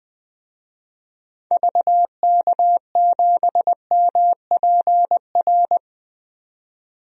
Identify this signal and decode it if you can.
Morse code, heard as a keyed tone, about 20 wpm, 700 Hz. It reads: VK7MPR
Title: DXCC-100 Call Signs - 6 Characters - 20wpm